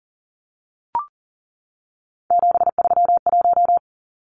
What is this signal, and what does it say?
Morse code, heard as a keyed tone, about 40 wpm, 700 Hz. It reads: 731
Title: Numbers - 3-Digit 40wpm